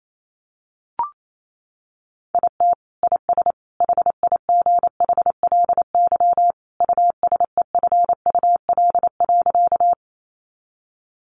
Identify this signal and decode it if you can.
Morse code, heard as a keyed tone, about 28 wpm, 700 Hz. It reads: IT IS HIGHLY USEFUL.
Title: Sentences from Top 1100 Words 28wpm